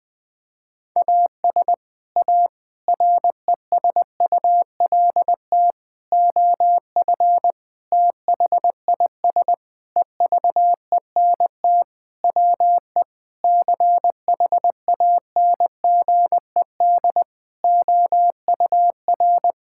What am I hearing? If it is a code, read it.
Morse code, heard as a keyed tone, about 20 wpm, 700 Hz. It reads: AS A RESULT OF THIS EVENT WE CHANGED OUR
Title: Sentences with 5-Word Phrases - 20wpm